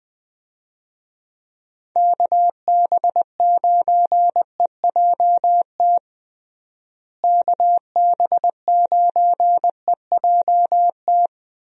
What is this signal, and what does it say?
Morse code, heard as a keyed tone, about 20 wpm, 700 Hz. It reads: KB9EJT KB9EJT
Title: USA Call Signs - 6 Characters - Repeated 3x - 20wpm